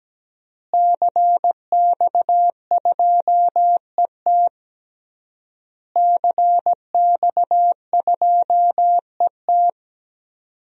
Morse code, heard as a keyed tone, about 17 wpm, 700 Hz. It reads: CX2ET CX2ET
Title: DXCC-100 Call Signs - 5 Characters - Repeated 3x - 17wpm